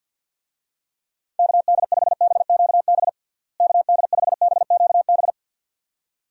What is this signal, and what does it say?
Morse code, heard as a keyed tone, about 51 wpm, 700 Hz. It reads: KD5BQB KD5BQB
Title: USA Call Signs - 6 Characters - Repeated 2x - 50wpm